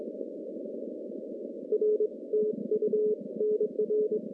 Morse code, heard as a keyed tone, about 22 wpm, 430 Hz. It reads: R TU NR
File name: contests / CW-SS-k5tr- -0400